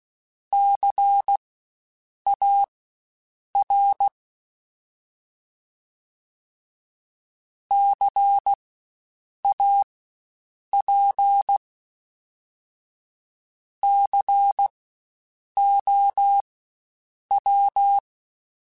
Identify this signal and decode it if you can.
Morse code, heard as a keyed tone, about 16 wpm, spaced as for 7 wpm, 800 Hz. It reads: CAR CAP COW
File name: Spaced Alphas / Spaced Alphas mp3s / Cs sp @15